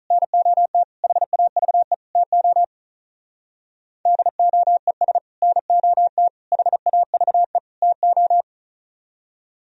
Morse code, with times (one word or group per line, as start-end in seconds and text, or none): 0.1-0.8 NOT
1.0-1.9 HAVE
2.2-2.6 TO
4.1-5.2 DOES
5.4-6.3 NOT
6.5-7.6 HAVE
7.8-8.4 TO